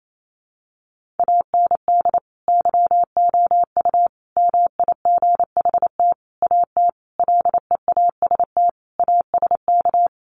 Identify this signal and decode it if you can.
Morse code, heard as a keyed tone, about 28 wpm, 700 Hz. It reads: AND YOU MIGHT AT LEAST ASK
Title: Sentences from Top 1500 Words 28wpm